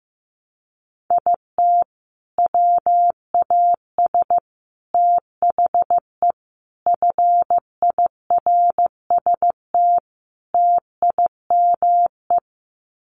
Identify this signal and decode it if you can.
Morse code, heard as a keyed tone, about 15 wpm, 700 Hz. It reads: IT WAS THE FIRST TIME
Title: Sets of 5 Words 15wpm